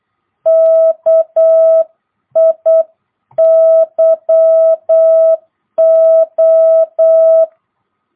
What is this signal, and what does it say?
Morse code, heard as a keyed tone, about 8 wpm, 660 Hz. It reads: KIYO
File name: jh8cbh / 2022cw